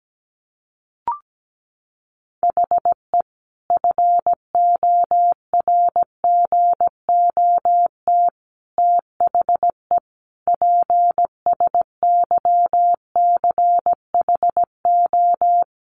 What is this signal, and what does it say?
Morse code, heard as a keyed tone, about 17 wpm, 700 Hz. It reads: HE FORGOT THE PSYCHO
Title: Sentences from Top 1400 Words 17wpm